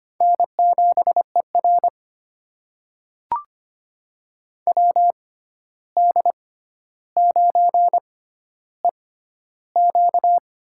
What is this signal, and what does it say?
Morse code, heard as a keyed tone, about 25 wpm, spaced as for 12 wpm, 700 Hz. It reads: N7ER WD9EQ
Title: USA Call Signs - 6x Character Spacing - 25wpm